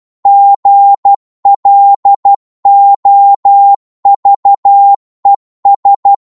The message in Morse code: GLOVES